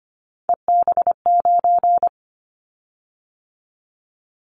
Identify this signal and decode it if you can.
Morse code, heard as a keyed tone, about 25 wpm, 700 Hz. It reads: EB9
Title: Callsign Prefixes for Top 100 DXCC - 25wpm